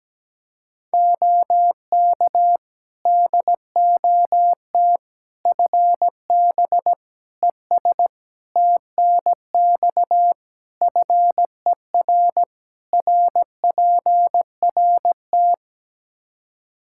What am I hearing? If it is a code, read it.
Morse code, heard as a keyed tone, about 17 wpm, 700 Hz. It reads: OK DOT FB ES TNX FER RPRT
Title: Practice QSO - Exchange 2 - Info - 17wpm